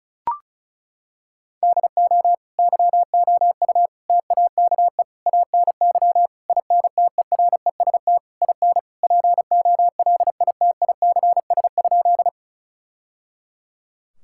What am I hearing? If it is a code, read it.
Morse code, heard as a keyed tone, about 36 wpm, 700 Hz. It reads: DO YOU TAKE ANY INTEREST IN POLITICS?
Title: Sentences from Top 1000 Words - Review Part 6 - 35wpm